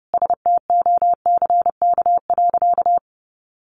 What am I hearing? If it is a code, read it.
Morse code, heard as a keyed tone, about 30 wpm, 700 Hz. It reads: STOCK.
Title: Sentences from Top 1300 Words 30wpm